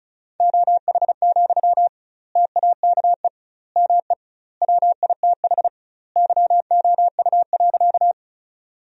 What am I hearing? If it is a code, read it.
Morse code, heard as a keyed tone, about 36 wpm, 700 Hz. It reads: OH, TAKE ME WITH YOU.
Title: 2x Word-Spacing - Sentences from Top 500 Words - 35wpm